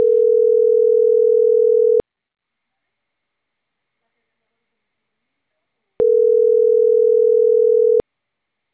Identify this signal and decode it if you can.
Ringback tone (North America), 440 + 480 Hz.